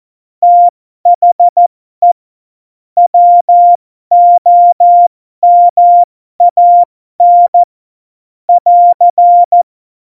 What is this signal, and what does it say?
Morse code, heard as a keyed tone, about 14 wpm, spaced as for 12 wpm, 700 Hz. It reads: THE WOMAN <AR>
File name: Morsekurs / Witze